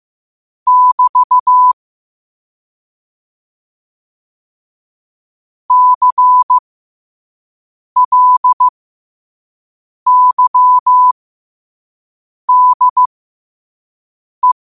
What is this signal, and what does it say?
Morse code, heard as a keyed tone, about 15 wpm, spaced as for 5 wpm, 1000 Hz. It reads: <BT> CLYDE
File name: qotdfiles / 5wpm